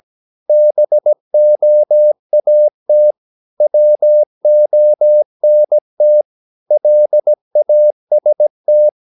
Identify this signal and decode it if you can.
Morse code, heard as a keyed tone, about 17 wpm, 600 Hz. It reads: BOAT WONT LAST
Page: Romanen Never af Ken Follet på engelsk.